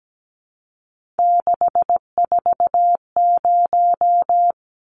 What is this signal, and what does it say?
Morse code, heard as a keyed tone, about 17 wpm, 700 Hz. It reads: 640
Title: Numbers - 3-Digit 17wpm